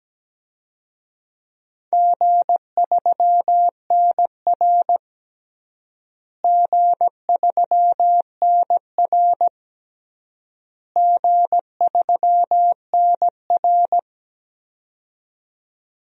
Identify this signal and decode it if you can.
Morse code, heard as a keyed tone, about 17 wpm, 700 Hz. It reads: G3NR G3NR G3NR
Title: DXCC-100 Call Signs - 4 Characters - Repeated 3x - 17wpm